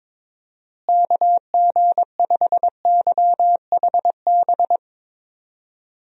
Morse code, heard as a keyed tone, about 21 wpm, 700 Hz. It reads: KG5YHB
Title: USA Call Signs - 6 Characters - Repeated 2x - 22wpm